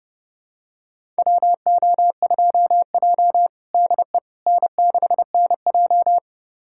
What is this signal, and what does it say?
Morse code, heard as a keyed tone, about 30 wpm, 700 Hz. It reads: WO2J DE N6NJ
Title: USA Callsign DE Callsign - 4 Characters - 30wpm